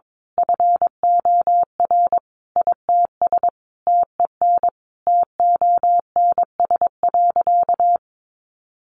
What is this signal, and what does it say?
Morse code, heard as a keyed tone, about 22 wpm, 700 Hz. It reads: FOR ITS TEN TONS.
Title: A Sound Of Thunder 22wpm